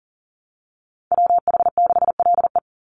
Morse code, heard as a keyed tone, about 40 wpm, 700 Hz. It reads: WH6LE
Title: NA SOTA Activator Callsigns 40wpm